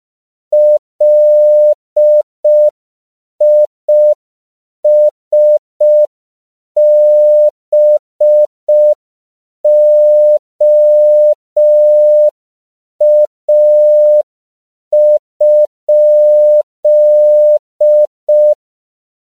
Morse code, heard as a keyed tone, about 5 wpm, 600 Hz. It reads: LISBOA?
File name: CW / 5PPM